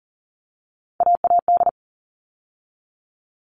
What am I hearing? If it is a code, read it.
Morse code, heard as a keyed tone, about 40 wpm, 700 Hz. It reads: AAD
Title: All Cut Numbers - 3-Digit 40wpm